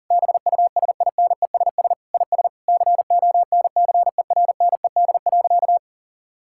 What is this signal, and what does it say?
Morse code, heard as a keyed tone, about 40 wpm, 690 Hz. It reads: BUSINESS IS CONCERNED.
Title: Sentences from Top 1000 Words - Review Part 6 - 40wpm